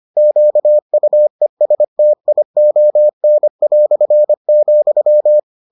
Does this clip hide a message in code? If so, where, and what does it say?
Yes: Morse code from 0.2 to 5.4 s, heard as a keyed tone, about 25 wpm, 600 Hz. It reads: QUESTION",